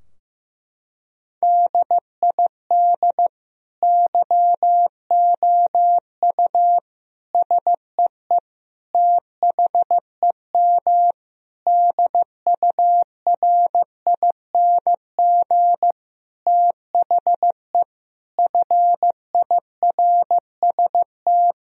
Morse code, heard as a keyed tone, about 15 wpm, 700 Hz. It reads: DID YOU SEE THEM DURING THE FIRST
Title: Sentences from Top 1000 Words - Review Part 2 - 15wpm